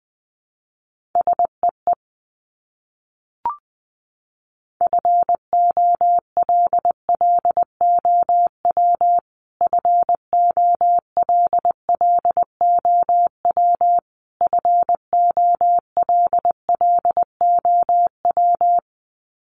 Morse code, heard as a keyed tone, about 20 wpm, 700 Hz. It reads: SEE FOLLOW FOLLOW FOLLOW
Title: Top 500 Words Repeated 20wpm